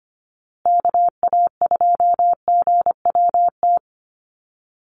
Morse code, heard as a keyed tone, about 25 wpm, 700 Hz. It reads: KA2GWT
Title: USA Call Signs Repeated 2x 25wpm